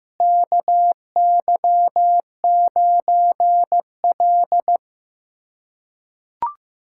Morse code, heard as a keyed tone, about 15 wpm, 700 Hz. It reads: KY9L
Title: USA Callsign DE Callsign 15wpm